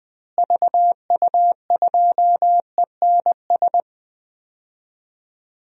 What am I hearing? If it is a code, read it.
Morse code, heard as a keyed tone, about 20 wpm, 700 Hz. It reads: VU2ENS
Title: DXCC-100 Call Signs - 6 Characters - Repeated 3x - 20wpm